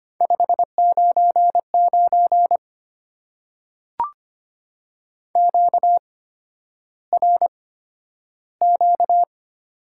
Morse code, heard as a keyed tone, about 25 wpm, spaced as for 11 wpm, 700 Hz. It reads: 599 QRQ
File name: Top 90 QSO Elements - 8x Character Spacing - 25wpm